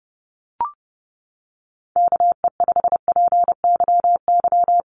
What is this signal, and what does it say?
Morse code, heard as a keyed tone, about 30 wpm, 700 Hz. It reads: KE5PYY
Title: USA Call Signs Repeated 2x 30wpm